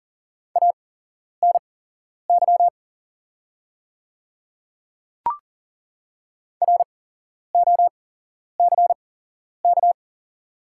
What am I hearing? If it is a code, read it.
Morse code, heard as a keyed tone, about 40 wpm, spaced as for 11 wpm, 700 Hz. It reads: ANY ROCK
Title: 8x Character Spacing - Top 500 Words - 40wpm